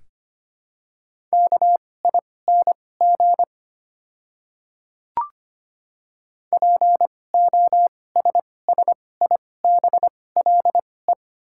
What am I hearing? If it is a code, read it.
Morse code, heard as a keyed tone, about 25 wpm, spaced as for 18 wpm, 700 Hz. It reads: KING POSSIBLE
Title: 2x Character Spacing - Top 500 Words - 25wpm